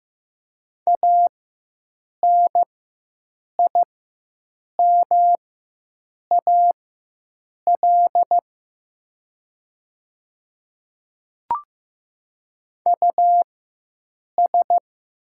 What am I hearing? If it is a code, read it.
Morse code, heard as a keyed tone, about 15 wpm, spaced as for 7 wpm, 700 Hz. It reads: ANIMAL US
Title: 4x Character Spacing - Top 300 Words - 15wpm